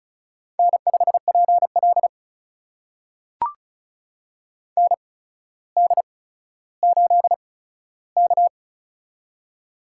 Morse code, heard as a keyed tone, about 34 wpm, spaced as for 15 wpm, 700 Hz. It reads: N5PL ND8K
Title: USA Call Signs - 4 Characters - 8x Character Spacing - 35wpm